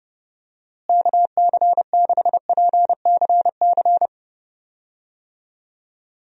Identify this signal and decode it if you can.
Morse code, heard as a keyed tone, about 30 wpm, 700 Hz. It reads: KC6PCC
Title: USA Call Signs - 6 Characters - Repeated 3x - 30wpm